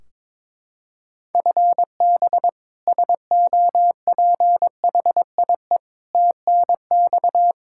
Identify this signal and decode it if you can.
Morse code, heard as a keyed tone, about 22 wpm, 700 Hz. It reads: FB SOPHIE TNX
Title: K1USN SST QSO Practice 22wpm